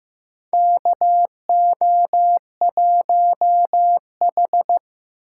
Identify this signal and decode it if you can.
Morse code, heard as a keyed tone, about 15 wpm, 700 Hz. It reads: KO1H
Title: USA Callsign DE Callsign - 4 Characters - 15wpm